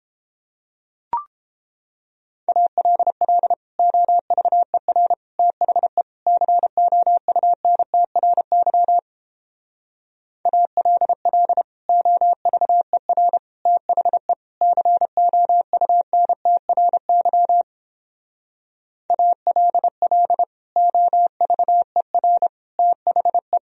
Morse code, heard as a keyed tone, about 31 wpm, 700 Hz. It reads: ALL OVER THE COUNTRY ALL OVER THE COUNTRY ALL OVER THE